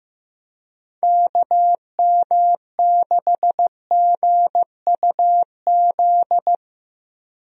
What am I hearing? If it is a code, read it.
Morse code, heard as a keyed tone, about 15 wpm, 700 Hz. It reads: KM6GUZ
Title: USA Call Signs - 6 Characters - Repeated 2x - 15wpm